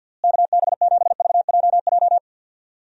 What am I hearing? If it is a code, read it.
Morse code, heard as a keyed tone, about 48 wpm, 700 Hz. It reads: KB7VJJ
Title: USA Call Signs - 6 Characters - Repeated 3x - 50wpm